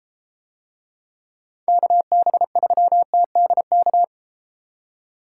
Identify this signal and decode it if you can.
Morse code, heard as a keyed tone, about 33 wpm, 700 Hz. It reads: KB3TDK